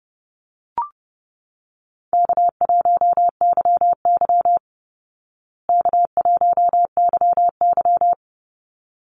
Morse code, heard as a keyed tone, about 30 wpm, 700 Hz. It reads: K1YY K1YY
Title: Easy Call Signs Repeated 2x - 30wpm